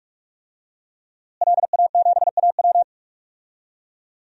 Morse code, heard as a keyed tone, about 46 wpm, 700 Hz. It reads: LA7AW
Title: DXCC-100 Call Signs - 5 Characters - Repeated 3x - 45wpm